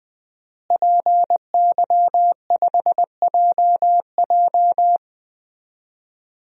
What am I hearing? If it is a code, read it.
Morse code, heard as a keyed tone, about 21 wpm, 700 Hz. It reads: PY5JJ
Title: DXCC-100 Call Signs - 5 Characters - Repeated 2x - 20wpm